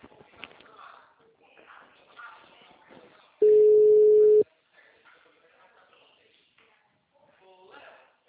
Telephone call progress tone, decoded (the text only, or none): ringback tone